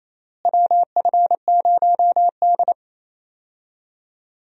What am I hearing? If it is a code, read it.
Morse code, heard as a keyed tone, about 28 wpm, 700 Hz. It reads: WF0D
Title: USA Call Signs - 4 Characters - Repeated 3x - 28wpm